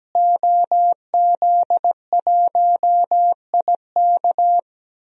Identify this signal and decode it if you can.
Morse code, heard as a keyed tone, about 17 wpm, 700 Hz. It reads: OZ1IK
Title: DXCC-100 Call Signs - 5 Characters - Repeated 2x - 17wpm